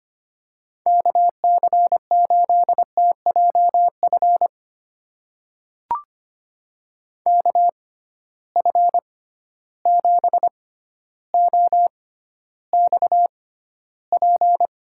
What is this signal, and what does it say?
Morse code, heard as a keyed tone, about 25 wpm, spaced as for 13 wpm, 700 Hz. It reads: KC8TJF KF7OXP
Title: USA Call Signs - 6 Characters - 6x Character Spacing - 25wpm